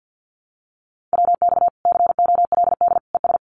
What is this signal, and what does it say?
Morse code, heard as a keyed tone, about 48 wpm, 700 Hz. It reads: WX COLD ES
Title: Common QSO Phrases 50wpm